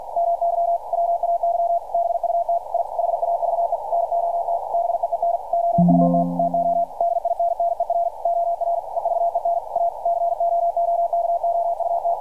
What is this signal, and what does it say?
Morse code, heard as a keyed tone, about 52 wpm, 680 Hz. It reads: MY WAY BUT I HAVE NOT BEEN OUT TO DINNER OR EVEN A COOKOUT
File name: archived / CRSnet